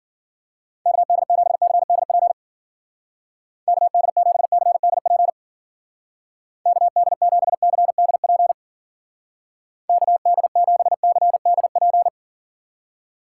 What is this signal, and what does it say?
Morse code, heard as a keyed tone, about 50 wpm, 700 Hz. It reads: KD7CDP KD7CDP KD7CDP KD7CDP
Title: Speed Racing - USA 2x3 Call Signs 40wpm